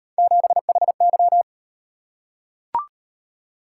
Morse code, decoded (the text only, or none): 7HY